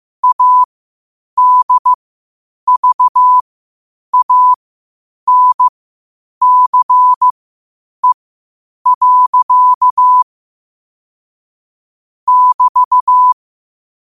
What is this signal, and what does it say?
Morse code, heard as a keyed tone, about 15 wpm, spaced as for 8 wpm, 1000 Hz. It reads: ADVANCE. <BT>
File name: qotdfiles / 8wpm